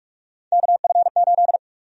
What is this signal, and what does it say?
Morse code, heard as a keyed tone, about 46 wpm, 700 Hz. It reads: KF8
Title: USA Call Sign Prefixes - 45wpm